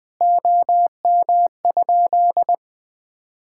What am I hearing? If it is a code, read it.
Morse code, heard as a keyed tone, about 20 wpm, 700 Hz. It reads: OM?
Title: Common QSO Phrases - 2x Word-Spacing - 20wpm